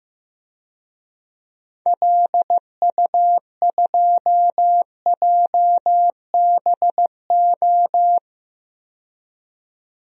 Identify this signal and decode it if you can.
Morse code, heard as a keyed tone, about 15 wpm, 700 Hz. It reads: LU2JBO
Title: DXCC-100 Call Signs - 6 Characters - 15wpm